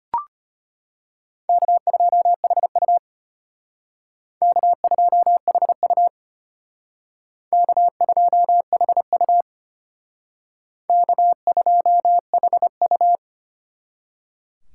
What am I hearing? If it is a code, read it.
Morse code, heard as a keyed tone, about 32 wpm, 700 Hz. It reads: K2HU K2HU K2HU K2HU